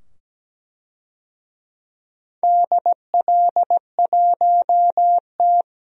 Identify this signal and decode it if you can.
Morse code, heard as a keyed tone, about 17 wpm, 700 Hz. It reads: DL1T
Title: DXCC-100 Call Signs - 4 Characters - 17wpm